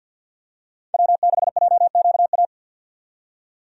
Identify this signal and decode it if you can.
Morse code, heard as a keyed tone, about 50 wpm, 700 Hz. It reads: W6JQA
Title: USA Call Signs - 5 Characters - 50wpm